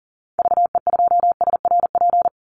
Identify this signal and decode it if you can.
Morse code, heard as a keyed tone, about 37 wpm, 700 Hz. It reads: VE2SRP
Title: Call Signs - 40wpm